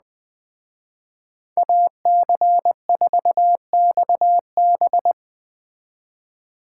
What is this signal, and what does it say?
Morse code, heard as a keyed tone, about 20 wpm, 700 Hz. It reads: AC4XB